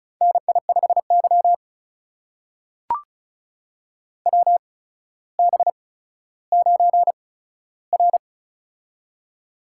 Morse code, heard as a keyed tone, about 36 wpm, spaced as for 15 wpm, 700 Hz. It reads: NI5Y WB9R
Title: USA Call Signs - 8x Character Spacing - 35wpm